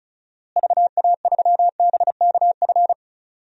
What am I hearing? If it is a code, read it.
Morse code, heard as a keyed tone, about 36 wpm, 700 Hz. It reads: VA3BKF
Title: DXCC-100 Call Signs - 6 Characters - Repeated 3x - 35wpm